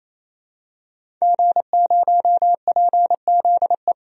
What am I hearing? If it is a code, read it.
Morse code, heard as a keyed tone, about 28 wpm, 700 Hz. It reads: G0PZE
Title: DXCC-100 Call Signs - 5 Characters - 28wpm